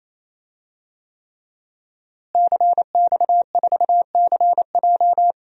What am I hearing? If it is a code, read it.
Morse code, heard as a keyed tone, about 27 wpm, 700 Hz. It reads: CX4CJ
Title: DXCC-100 Call Signs - 5 Characters - Repeated 3x - 28wpm